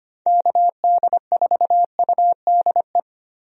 Morse code, heard as a keyed tone, about 25 wpm, 700 Hz. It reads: KD4UDE
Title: USA Call Signs - 6 Characters - 25wpm